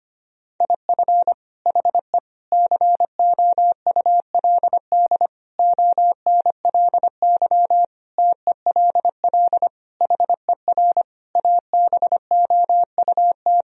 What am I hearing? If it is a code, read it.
Morse code, heard as a keyed tone, about 25 wpm, 700 Hz. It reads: IF HE COULD ONLY TELL HER ABOUT
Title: Sentences from Top 1000 Words - Review Part 1 - 25wpm